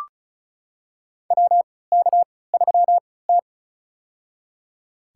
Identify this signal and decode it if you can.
Morse code, heard as a keyed tone, about 34 wpm, spaced as for 20 wpm, 700 Hz. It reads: WK3T
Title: USA Call Signs - 4 Characters - 3x Character Spacing - 35wpm